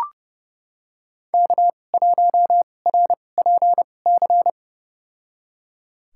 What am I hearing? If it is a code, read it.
Morse code, heard as a keyed tone, about 30 wpm, spaced as for 22 wpm, 700 Hz. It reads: K1RPC
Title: USA Call Signs - 5 Characters - 2x Character Spacing - 30wpm